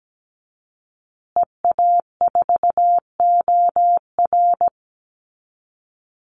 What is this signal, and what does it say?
Morse code, heard as a keyed tone, about 17 wpm, 700 Hz. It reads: EA4OR
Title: Call Signs Repeated 2x - 17wpm